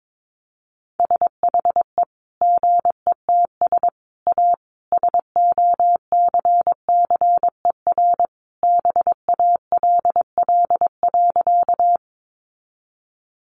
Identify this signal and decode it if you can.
Morse code, heard as a keyed tone, about 22 wpm, 700 Hz. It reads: SHE GETS A SOCCER BALL.
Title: Easy Phrases 22wpm